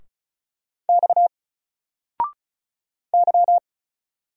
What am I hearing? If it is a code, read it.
Morse code, heard as a keyed tone, about 36 wpm, 700 Hz. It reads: X Y
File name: Single-Letter-Number-35wpm